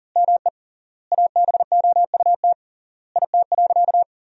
Morse code, heard as a keyed tone, about 40 wpm, 700 Hz. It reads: ME ABOUT IT.
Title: 3x Word-Spacing - Sentences from Top 200 Words - 40wpm